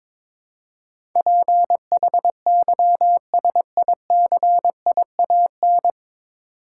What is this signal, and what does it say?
Morse code, heard as a keyed tone, about 23 wpm, 700 Hz. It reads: PHYSICIAN
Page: Morse Code Ninja practice 22wpm